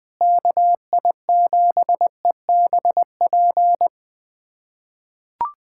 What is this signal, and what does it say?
Morse code, heard as a keyed tone, about 20 wpm, 700 Hz. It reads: KI7EBP
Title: USA Callsign DE Callsign - 6 Characters - 20wpm